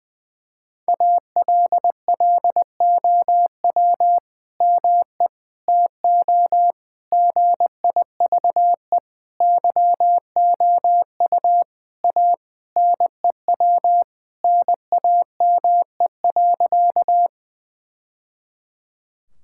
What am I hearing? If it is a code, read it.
Morse code, heard as a keyed tone, about 20 wpm, 700 Hz. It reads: ALLOW ME TO GIVE YOU A NEW NAME.
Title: Sentences from Top 1000 Words - Review Part 5 - 20wpm